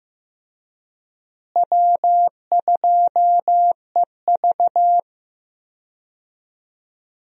Morse code, heard as a keyed tone, about 15 wpm, 700 Hz. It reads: W2EV
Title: USA Call Signs - 4 Characters - Repeated 2x - 15wpm